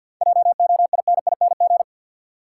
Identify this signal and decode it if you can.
Morse code, heard as a keyed tone, about 52 wpm, 700 Hz. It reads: JOINING
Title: Sets of 5 Words - New Word Drill - 50wpm